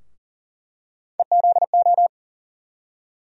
Morse code, heard as a keyed tone, about 40 wpm, 700 Hz. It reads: EZO
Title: USA Call Sign Postfixes - 3 Letters 40wpm